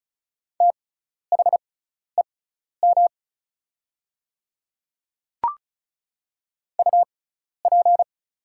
Morse code, heard as a keyed tone, about 36 wpm, spaced as for 12 wpm, 700 Hz. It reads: THEM UP
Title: 6x Character Spacing - Top 100 Words - 35wpm